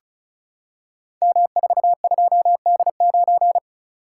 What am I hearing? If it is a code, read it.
Morse code, heard as a keyed tone, about 36 wpm, 690 Hz. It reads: M42D9
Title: Code Groups - 5 Letter:Numbers - 35wpm